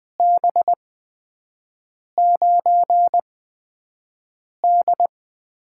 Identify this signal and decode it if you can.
Morse code, heard as a keyed tone, about 20 wpm, spaced as for 5 wpm, 700 Hz. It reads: B9D
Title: USA Call Signs - 4 Characters - 8x Character Spacing - 20wpm